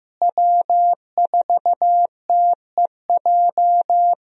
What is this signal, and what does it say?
Morse code, heard as a keyed tone, about 15 wpm, 700 Hz. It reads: W4TEJ
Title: USA Call Signs - 5 Characters - Repeated 3x - 15wpm